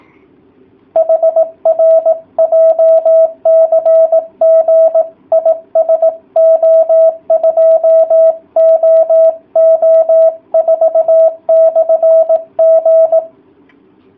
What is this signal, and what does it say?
Morse code, heard as a keyed tone, about 18 wpm, 650 Hz. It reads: HR JCG IS O2OO4/G